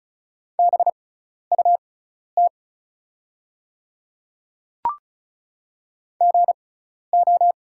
Morse code, heard as a keyed tone, about 36 wpm, spaced as for 12 wpm, 700 Hz. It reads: BUT GO
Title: 6x Character Spacing - Top 100 Words - 35wpm